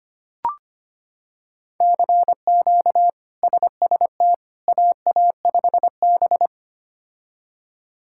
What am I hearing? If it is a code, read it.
Morse code, heard as a keyed tone, about 25 wpm, 700 Hz. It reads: CQ SST AA5B
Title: K1USN SST QSO Practice - 25wpm